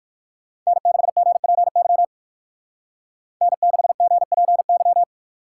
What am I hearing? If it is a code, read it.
Morse code, heard as a keyed tone, about 53 wpm, 700 Hz. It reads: N6GPY N6GPY
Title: Speed Racing - USA 1x3 Call Signs 45wpm